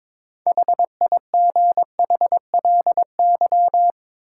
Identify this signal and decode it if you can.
Morse code, heard as a keyed tone, about 22 wpm, 700 Hz. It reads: HIGHLY
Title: Sentences from Top 1100 Words - New Word Drill Alt - 22wpm